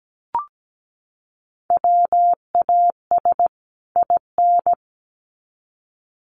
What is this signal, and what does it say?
Morse code, heard as a keyed tone, about 17 wpm, 700 Hz. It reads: WAS IN
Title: Sets of 2 Words 17wpm